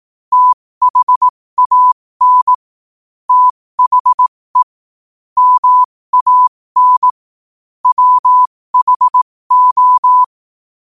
Morse code, heard as a keyed tone, about 18 wpm, spaced as for 15 wpm, 1000 Hz. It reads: THAN THE MAN WHO
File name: qotdfiles / 15wpm